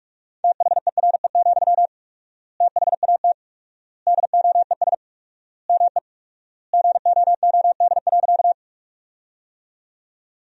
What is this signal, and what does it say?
Morse code, heard as a keyed tone, about 46 wpm, 700 Hz. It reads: THERE, THAT DOES ME GOOD.
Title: 4x Word-Spacing - Sentences from Top 200 Words - 45wpm